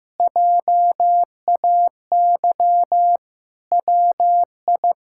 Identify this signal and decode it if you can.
Morse code, heard as a keyed tone, about 15 wpm, 700 Hz. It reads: JAY WI
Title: K1USN SST QSO Practice - 15wpm